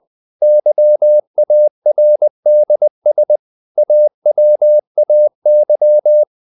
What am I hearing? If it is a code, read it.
Morse code, heard as a keyed tone, about 20 wpm, 600 Hz. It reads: YARDS AWAY
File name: the old man and the sea part 2 20wpm